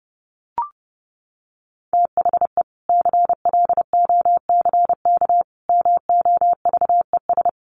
Morse code, heard as a keyed tone, about 30 wpm, 700 Hz. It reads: THE CLOCK MOVES
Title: Easy Phrases 30wpm